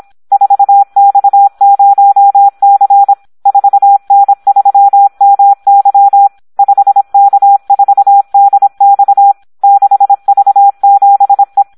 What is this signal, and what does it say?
Morse code, heard as a keyed tone, about 26 wpm, 800 Hz. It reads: 4X0C 4N3MY 5K4DX 6V7E
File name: Audio practica CW / SD 1 26